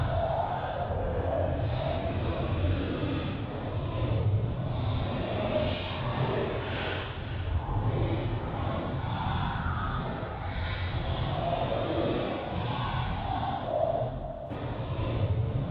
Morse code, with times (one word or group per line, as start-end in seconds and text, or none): none